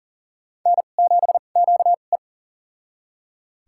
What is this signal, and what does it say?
Morse code, heard as a keyed tone, about 37 wpm, spaced as for 28 wpm, 700 Hz. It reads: N7QE
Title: USA Call Signs - 4 Characters - 2x Character Spacing - 40wpm